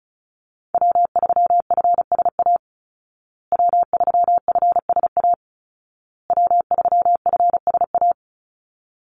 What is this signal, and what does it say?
Morse code, heard as a keyed tone, about 34 wpm, 700 Hz. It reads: W3FSA W3FSA W3FSA
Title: Call Signs Repeated 2x - 35wpm